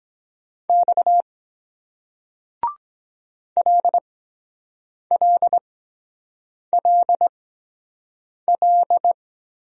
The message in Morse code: X LLLL

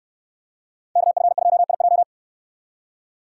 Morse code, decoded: KF2IJ

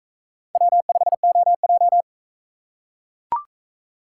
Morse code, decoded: W5OJ

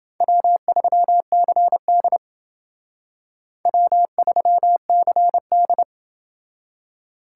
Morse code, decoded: W3CD W3CD

